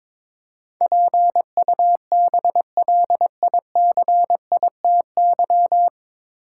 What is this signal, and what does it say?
Morse code, heard as a keyed tone, about 22 wpm, 700 Hz. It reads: PUBLICITY
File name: 9 Letter Words - First 3 Letters Given - 22wpm